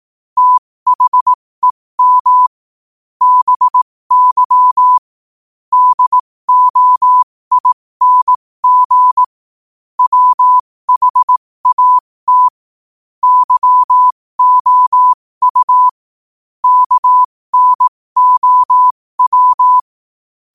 Morse code, heard as a keyed tone, about 18 wpm, spaced as for 15 wpm, 1000 Hz. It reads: THEM BY DOING WHAT YOU KNOW
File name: qotdfiles / 15wpm